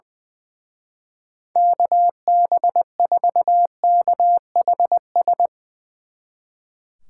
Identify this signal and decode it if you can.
Morse code, heard as a keyed tone, about 22 wpm, 700 Hz. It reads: KB4KHS
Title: Speed Racing - USA 2x3 Call Signs 20wpm